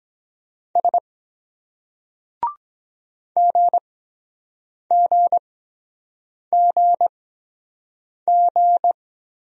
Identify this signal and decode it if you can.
Morse code, heard as a keyed tone, about 23 wpm, spaced as for 7 wpm, 700 Hz. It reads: S GGGG